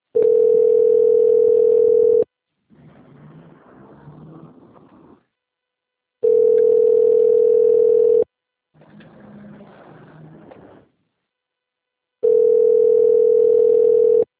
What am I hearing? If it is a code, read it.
Ringback tone (North America), 440 + 480 Hz.
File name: out- - - - - .10000